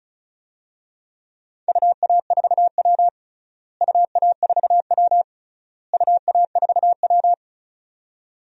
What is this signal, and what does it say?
Morse code, heard as a keyed tone, about 36 wpm, 700 Hz. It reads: UA4W UA4W UA4W
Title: DXCC-100 Call Signs - 4 Characters - Repeated 3x - 35wpm